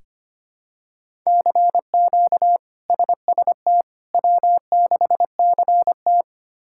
Morse code, heard as a keyed tone, about 25 wpm, 700 Hz. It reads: CQ SST W6CT
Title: K1USN SST QSO Practice - 25wpm